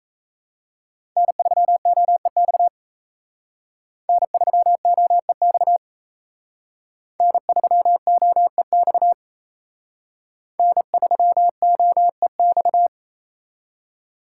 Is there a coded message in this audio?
Yes: Morse code from 1.2 to 12.9 s, heard as a keyed tone, about 35 wpm, 700 Hz. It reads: N3OEX N3OEX N3OEX N3OEX